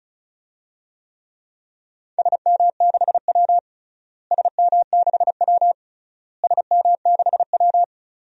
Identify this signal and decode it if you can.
Morse code, heard as a keyed tone, about 34 wpm, 700 Hz. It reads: SM6W SM6W SM6W
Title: DXCC-100 Call Signs - 4 Characters - Repeated 3x - 35wpm